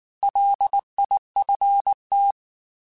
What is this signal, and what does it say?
Morse code, heard as a keyed tone, about 19 wpm, 800 Hz. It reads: LIFT